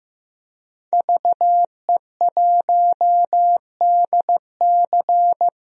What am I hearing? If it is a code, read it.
Morse code, heard as a keyed tone, about 15 wpm, 700 Hz. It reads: VE1DC
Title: DXCC-100 Call Signs - 5 Characters - 15wpm